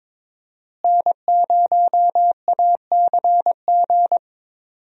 Morse code, heard as a keyed tone, about 22 wpm, 700 Hz. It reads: N0ACG
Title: USA Call Signs - 5 Characters - Repeated 3x - 22wpm